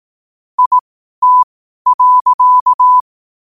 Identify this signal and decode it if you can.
Morse code, heard as a keyed tone, about 18 wpm, spaced as for 12 wpm, 1000 Hz. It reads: IT.
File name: qotdfiles / 12wpm